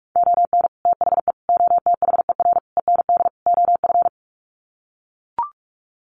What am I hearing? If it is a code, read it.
Morse code, heard as a keyed tone, about 44 wpm, 700 Hz. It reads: ON THE OTHER END OF
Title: Sets of 5 Words Encore 45wpm